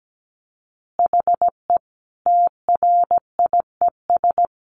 Morse code, heard as a keyed tone, about 17 wpm, 700 Hz. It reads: HE TRIES
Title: Easy Phrases 17wpm